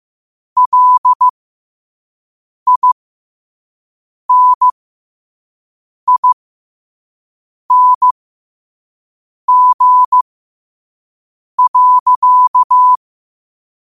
Morse code, heard as a keyed tone, about 15 wpm, spaced as for 5 wpm, 1000 Hz. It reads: LINING.
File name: qotdfiles / 5wpm